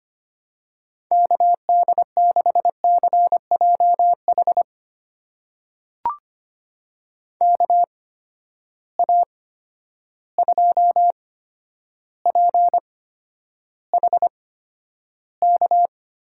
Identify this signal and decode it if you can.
Morse code, heard as a keyed tone, about 25 wpm, spaced as for 11 wpm, 700 Hz. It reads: KD6CJH KA2PHK
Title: USA Call Signs - 6 Characters - 8x Character Spacing - 25wpm